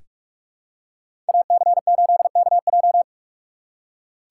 Morse code, heard as a keyed tone, about 43 wpm, 700 Hz. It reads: AC8KJ
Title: USA Call Signs - 2x Character Spacing - 45wpm